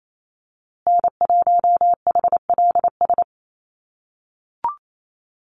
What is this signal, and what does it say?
Morse code, heard as a keyed tone, about 28 wpm, 700 Hz. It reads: N1HLS
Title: USA Call Signs Repeated 3x 28wpm